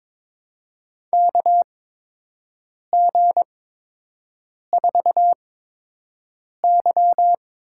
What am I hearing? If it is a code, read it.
Morse code, heard as a keyed tone, about 22 wpm, spaced as for 6 wpm, 700 Hz. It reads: KG4Y
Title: USA Call Signs - 6 Characters - 8x Character Spacing - 22wpm